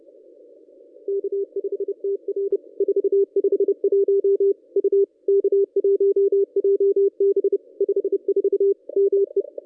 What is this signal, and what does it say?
Morse code, heard as a keyed tone, about 30 wpm, 440 Hz. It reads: K5TR 451 U K1JB 54 ME
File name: contests / ss-cw-k5tr-op-n5rz -1200